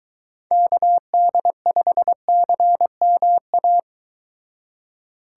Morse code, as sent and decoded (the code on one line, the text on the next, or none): -.- -.. ..... -.-. -- .-
KD5CMA